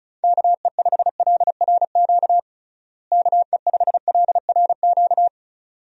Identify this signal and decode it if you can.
Morse code, heard as a keyed tone, about 34 wpm, 700 Hz. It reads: KE5LRQ KE5LRQ
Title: USA Call Signs - 6 Characters - Repeated 2x - 35wpm